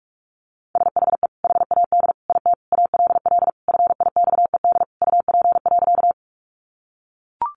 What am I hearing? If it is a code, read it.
Morse code, heard as a keyed tone, about 44 wpm, 700 Hz. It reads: SHE HAD IT ALL FIXED UP.
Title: Sentences from Top 1100 Words 45wpm